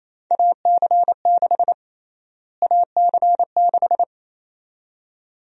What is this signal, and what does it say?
Morse code, heard as a keyed tone, about 28 wpm, 700 Hz. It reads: AC6 AC6
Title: USA Call Sign Prefixes - Repeated 2x - 28wpm